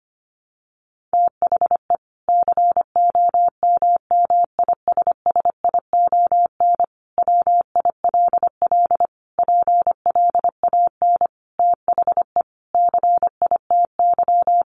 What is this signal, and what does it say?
Morse code, heard as a keyed tone, about 25 wpm, 700 Hz. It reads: THE COMMISSION WILL PLAN THE CITY
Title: Sentences from Top 1400 Words 25wpm